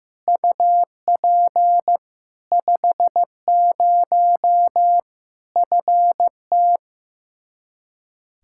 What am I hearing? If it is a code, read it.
Morse code, heard as a keyed tone, about 15 wpm, 700 Hz. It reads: UP 50 FT
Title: Practice QSO - Exchange 2 - Info - 15wpm